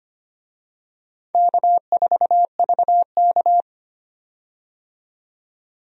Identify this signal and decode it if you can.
Morse code, heard as a keyed tone, about 25 wpm, 700 Hz. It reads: K4VK
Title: USA Call Signs - 4 Characters - 25wpm